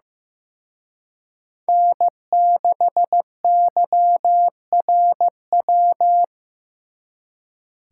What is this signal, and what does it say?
Morse code, heard as a keyed tone, about 18 wpm, 700 Hz. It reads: N6YRW
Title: Speed Racing - USA 1x3 Call Signs 15wpm